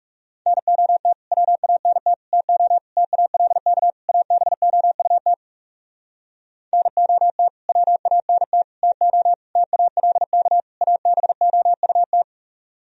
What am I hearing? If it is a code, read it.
Morse code, heard as a keyed tone, about 40 wpm, 700 Hz. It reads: NOT WANT TO TALK ABOUT NOT WANT TO TALK ABOUT